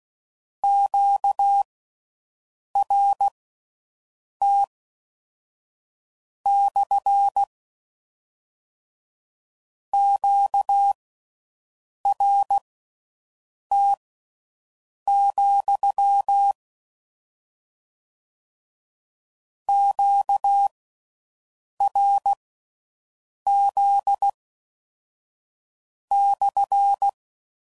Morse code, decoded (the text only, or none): QRT / QRT, QRZ /